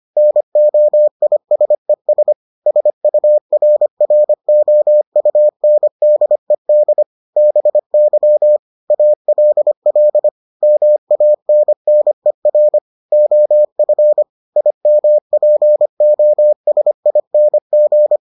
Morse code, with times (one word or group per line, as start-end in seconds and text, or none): 0.2-2.3 NOISES
2.7-7.0 SURROUNDED
7.4-8.6 BY
8.9-10.3 ALL
10.6-12.8 MANNER
13.1-14.2 OF
14.6-18.2 IMPOSING